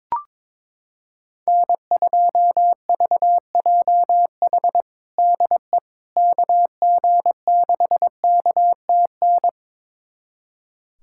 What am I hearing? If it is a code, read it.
Morse code, heard as a keyed tone, about 22 wpm, 700 Hz. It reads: N2VJH DE KG6KTN
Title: USA Callsign DE Callsign 22wpm